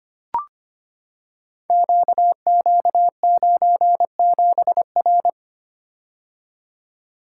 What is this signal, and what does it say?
Morse code, heard as a keyed tone, about 25 wpm, 700 Hz. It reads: QQ97R
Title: Code Groups - 5 Letter:Numbers - 25wpm